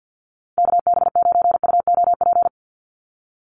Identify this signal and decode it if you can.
Morse code, heard as a keyed tone, about 50 wpm, 700 Hz. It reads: KB9UOP
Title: USA Call Signs 50wpm